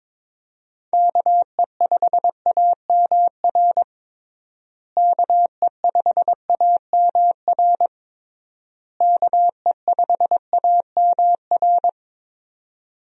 Morse code, heard as a keyed tone, about 22 wpm, 700 Hz. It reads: KE5AMR KE5AMR KE5AMR
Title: USA Call Signs - 6 Characters - Repeated 3x - 22wpm